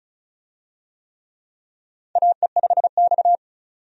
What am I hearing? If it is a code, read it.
Morse code, heard as a keyed tone, about 36 wpm, 700 Hz. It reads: AE5X
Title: USA Call Signs - 4 Characters - 35wpm